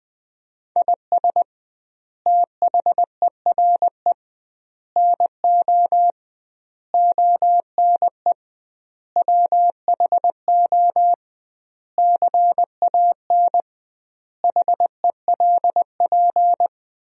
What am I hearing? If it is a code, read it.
Morse code, heard as a keyed tone, about 20 wpm, 700 Hz. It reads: IS THERE NO ONE WHO CAN HELP
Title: 2x Word-Spacing - Sentences from Top 200 Words - 20wpm